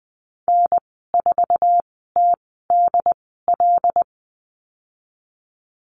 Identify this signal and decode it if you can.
Morse code, heard as a keyed tone, about 20 wpm, spaced as for 15 wpm, 700 Hz. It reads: N4TDL
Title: 088 Lesson 15g - Number 5 - Call Signs 20wpm